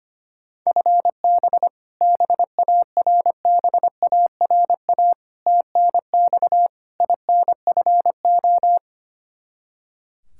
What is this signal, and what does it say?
Morse code, heard as a keyed tone, about 25 wpm, 700 Hz. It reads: FB BARBARA TNX INFO
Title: Practice QSO - Exchange 3 - More Info - 25wpm